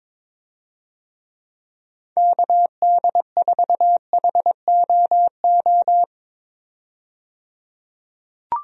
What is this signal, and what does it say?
Morse code, heard as a keyed tone, about 22 wpm, 700 Hz. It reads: KD4HOO
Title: USA Call Signs - 6 Characters - 22wpm